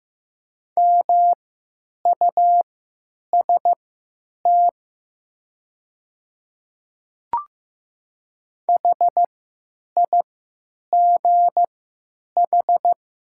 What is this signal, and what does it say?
Morse code, heard as a keyed tone, about 15 wpm, spaced as for 9 wpm, 700 Hz. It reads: MUST HIGH
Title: 3x Character Spacing - Top 200 Words - 15wpm